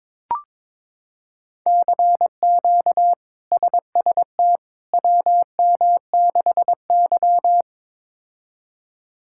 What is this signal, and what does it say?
Morse code, heard as a keyed tone, about 22 wpm, 700 Hz. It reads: CQ SST WM6Y
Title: K1USN SST QSO Practice - 22wpm